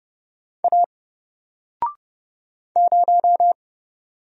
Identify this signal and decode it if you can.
Morse code, heard as a keyed tone, about 30 wpm, 700 Hz. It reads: A 0
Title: Single Letter-Number Alt 30wpm